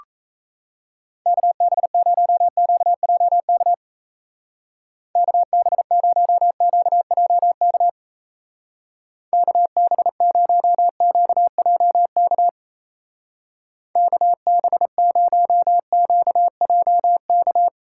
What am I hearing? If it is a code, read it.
Morse code, heard as a keyed tone, about 34 wpm, 700 Hz. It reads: KB0QJK KB0QJK KB0QJK KB0QJK
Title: Speed Racing - USA 2x3 Call Signs 28wpm